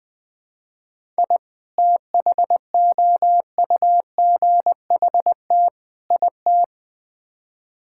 Morse code, heard as a keyed tone, about 20 wpm, 700 Hz. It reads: I THOUGHT IT
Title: Build A Phrase - 5-Word Phrases - 20wpm